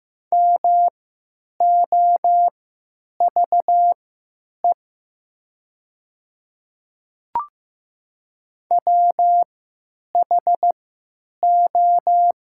Morse code, heard as a keyed tone, about 15 wpm, spaced as for 8 wpm, 700 Hz. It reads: MOVE WHO
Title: 3x Character Spacing - Top 200 Words - 15wpm